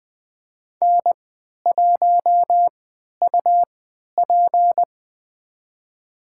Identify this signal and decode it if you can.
Morse code, heard as a keyed tone, about 20 wpm, spaced as for 11 wpm, 700 Hz. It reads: N1UP
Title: USA Call Signs - 4 Characters - 3x Character Spacing - 20wpm